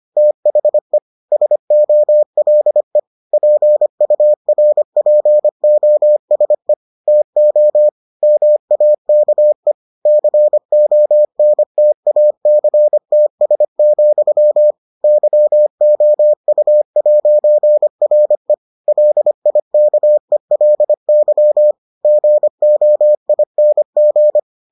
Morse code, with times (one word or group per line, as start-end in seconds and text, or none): 0.2-1.0 THE
1.3-3.0 SOLE
3.3-6.7 PURPOSE
7.1-7.9 TO
8.2-9.7 MAKE
10.1-14.7 CONTACTS,
15.0-18.5 YOU'RE
18.9-21.7 LIKELY
22.1-24.4 GOING